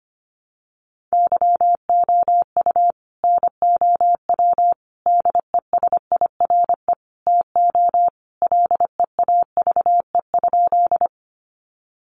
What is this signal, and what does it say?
Morse code, heard as a keyed tone, about 25 wpm, 700 Hz. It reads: YOU NOW DESIRE TO LEAVE?
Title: Sentences from Top 1100 Words 25wpm